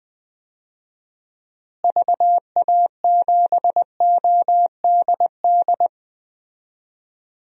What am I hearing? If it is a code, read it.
Morse code, heard as a keyed tone, about 20 wpm, 700 Hz. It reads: VA7ODD
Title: DXCC-100 Call Signs - 6 Characters - Repeated 3x - 20wpm